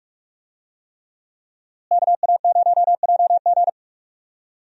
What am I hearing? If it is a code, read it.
Morse code, heard as a keyed tone, about 44 wpm, 700 Hz. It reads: KA0JG
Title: USA Call Signs - 5 Characters - Repeated 3x - 45wpm